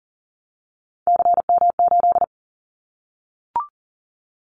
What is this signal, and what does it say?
Morse code, heard as a keyed tone, about 39 wpm, 700 Hz. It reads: CM8
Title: Callsign Prefixes for Top 100 DXCC - 40wpm